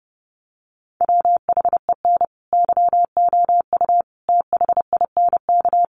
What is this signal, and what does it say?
Morse code, heard as a keyed tone, about 30 wpm, 700 Hz. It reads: WHEN YOU THINK
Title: Sets of 5 Words Encore2 30wpm